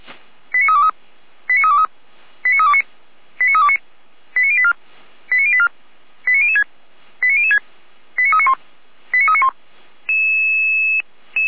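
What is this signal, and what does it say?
Five-tone selective call (ZVEI), heard as quick runs of five tones, tones of about 70 ms. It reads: 89323 89323 89329 89329 89084 89084 89006 89006 89391 89391
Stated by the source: ZVEI five-tone selective call received by SDR, from audio.f85360.feuerwehr-orlamuende.org, a fire brigade (Feuerwehr Orlamünde) server.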